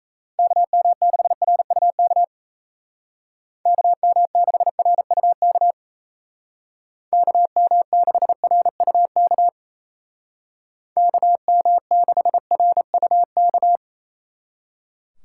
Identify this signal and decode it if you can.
Morse code, heard as a keyed tone, about 33 wpm, 700 Hz. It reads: KM6RUK KM6RUK KM6RUK KM6RUK
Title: Speed Racing - USA Call Signs 28wpm